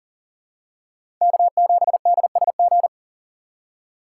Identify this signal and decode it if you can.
Morse code, heard as a keyed tone, about 40 wpm, 700 Hz. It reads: K7DSG
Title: USA Call Signs - 3x Character Spacing - 40wpm